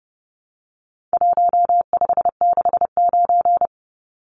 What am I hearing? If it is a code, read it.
Morse code, heard as a keyed tone, about 30 wpm, 700 Hz. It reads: 1569
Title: Numbers - Contest Style - 4-Digit 30wpm